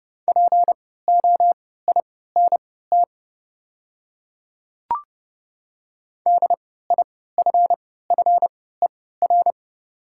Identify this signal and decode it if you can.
Morse code, heard as a keyed tone, about 32 wpm, spaced as for 17 wpm, 690 Hz. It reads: POINT DIFFER
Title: 3x Character Spacing - Top 200 Words - 30wpm